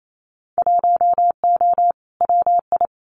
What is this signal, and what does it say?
Morse code, heard as a keyed tone, about 28 wpm, 700 Hz. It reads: 1O WI
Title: Winter Field Day Exchanges 28wpm